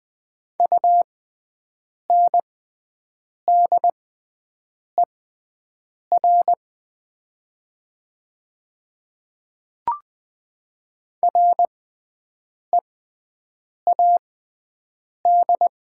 Morse code, heard as a keyed tone, about 20 wpm, spaced as for 7 wpm, 700 Hz. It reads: UNDER READ
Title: 6x Character Spacing - Top 200 Words - 20wpm